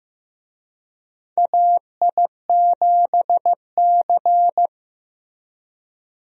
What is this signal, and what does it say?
Morse code, heard as a keyed tone, about 15 wpm, 700 Hz. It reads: AI7C
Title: USA Call Signs - 4 Characters - Repeated 2x - 15wpm